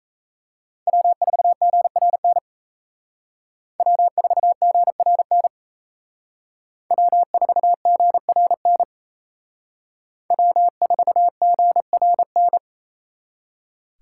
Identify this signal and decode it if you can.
Morse code, heard as a keyed tone, about 34 wpm, 700 Hz. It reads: W4GRN W4GRN W4GRN W4GRN
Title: Speed Racing - USA Call Signs 28wpm